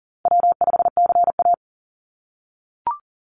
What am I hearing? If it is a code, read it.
Morse code, heard as a keyed tone, about 40 wpm, 700 Hz. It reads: W5CA
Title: USA Call Signs 40wpm